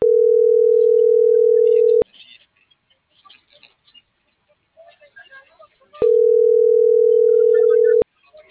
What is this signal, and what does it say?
Ringback tone (North America), 440 + 480 Hz.